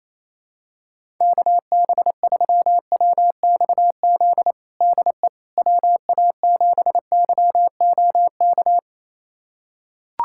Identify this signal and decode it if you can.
Morse code, heard as a keyed tone, about 27 wpm, 700 Hz. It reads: KB3WXZ DE WA7YOK
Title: USA Callsign DE Callsign - 6 Characters - 28wpm